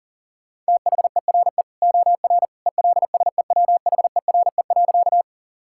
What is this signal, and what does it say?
Morse code, heard as a keyed tone, about 37 wpm, 700 Hz. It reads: THERE OR ELSEWHERE.
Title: Sentences from Top 1600 Words 40wpm